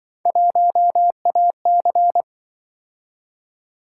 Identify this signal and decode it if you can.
Morse code, heard as a keyed tone, about 25 wpm, 700 Hz. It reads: 1AC
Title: Speed-Racing - Three Letter-Number Combinations - 20wpm